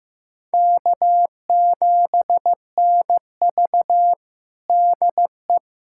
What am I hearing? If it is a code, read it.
Morse code, heard as a keyed tone, about 15 wpm, 700 Hz. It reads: K7NV DE
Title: USA Callsign DE Callsign - 4 Characters - 15wpm